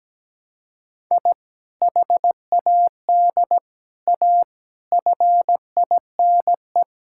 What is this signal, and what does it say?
Morse code, heard as a keyed tone, about 17 wpm, 700 Hz. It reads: I HAD A FINE
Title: Sentences from Top 1000 Words - Review Part 6 - 17wpm